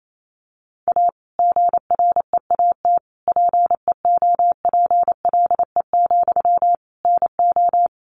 Morse code, heard as a keyed tone, about 28 wpm, 700 Hz. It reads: A GREAT PEOPLE, NO
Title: Sentences from Top 1500 Words 28wpm